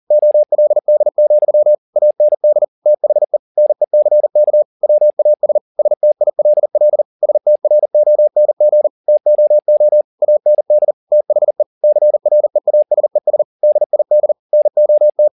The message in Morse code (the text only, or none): OLD, AND THE NECK WAS STILL STRONG TOO AND THE CREASES DID NOT